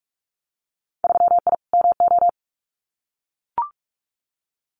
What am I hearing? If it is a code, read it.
Morse code, heard as a keyed tone, about 44 wpm, 700 Hz. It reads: 3I MO
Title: Winter Field Day Exchanges 45wpm